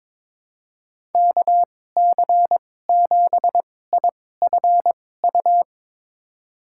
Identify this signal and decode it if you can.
Morse code, heard as a keyed tone, about 22 wpm, spaced as for 16 wpm, 700 Hz. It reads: KC7IFU
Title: USA Call Signs - 6 Characters - 2x Character Spacing - 22wpm